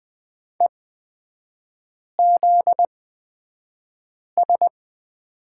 Morse code, heard as a keyed tone, about 20 wpm, spaced as for 5 wpm, 700 Hz. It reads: EZS
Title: 243 Lesson 39e - Letter z - Three Characters 20wpm